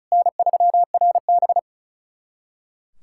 Morse code, heard as a keyed tone, about 34 wpm, 690 Hz. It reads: N3RB
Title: SOTA QSO Practice 35wpm